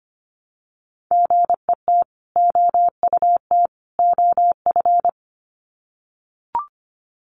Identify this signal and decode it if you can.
Morse code, heard as a keyed tone, about 25 wpm, 700 Hz. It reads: GET OUT OF
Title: Sets of 3 Words 25wpm